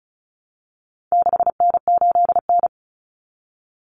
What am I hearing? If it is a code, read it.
Morse code, heard as a keyed tone, about 34 wpm, 700 Hz. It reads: 6N8N
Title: Common Cut Numbers - 4-Digit 35wpm